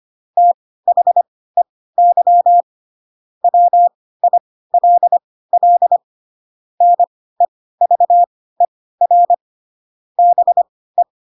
Morse code, heard as a keyed tone, about 25 wpm, spaced as for 16 wpm, 700 Hz. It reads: THEY WILL NEVER BE
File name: Pinocchio / Pinocchio ch4 10 25 ik1hge